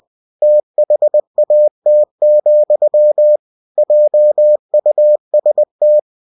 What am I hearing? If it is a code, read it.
Morse code, heard as a keyed tone, about 20 wpm, 600 Hz. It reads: THAT, JUST
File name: cw-books / winnie-the-pooh / winnie-20wpm-page-0002